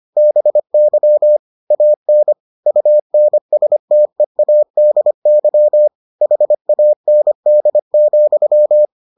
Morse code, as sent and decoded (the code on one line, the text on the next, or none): -... -.-- / .- -. / ..- -. ... - . .- -.. -.-- / .... .- -. -.. --..--
BY AN UNSTEADY HAND,